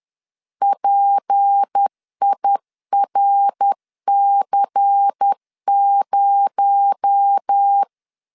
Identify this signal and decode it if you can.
Morse code, heard as a keyed tone, about 10 wpm, 800 Hz. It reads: PIRC0
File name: LEZIONI / 6 PKX0 / 6 PKX0 10wpm 2707